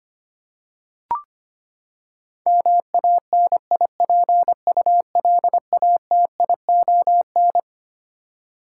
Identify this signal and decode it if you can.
Morse code, heard as a keyed tone, about 25 wpm, 700 Hz. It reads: MANIPULATION
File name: Twelve Letter Words - 25wpm